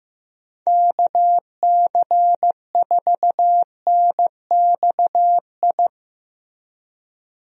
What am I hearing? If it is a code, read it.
Morse code, heard as a keyed tone, about 15 wpm, 700 Hz. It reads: KC4NXI
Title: USA Call Signs - 6 Characters - Repeated 2x - 15wpm